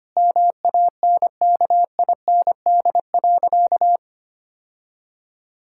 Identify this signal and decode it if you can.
Morse code, heard as a keyed tone, about 25 wpm, 700 Hz. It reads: MANKIND.
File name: practice / 25wpm / Sentences with 9-Word Phrases - 25wpm